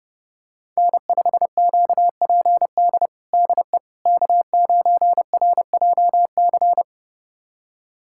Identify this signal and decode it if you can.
Morse code, heard as a keyed tone, about 30 wpm, 700 Hz. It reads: N5QPD DE K9RJC
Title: USA Callsign DE Callsign - 5 Characters - 30wpm